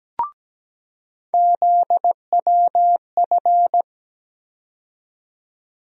Morse code, heard as a keyed tone, about 17 wpm, 700 Hz. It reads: ZWF
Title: USA Call Sign Postfixes - 3 Letters 17wpm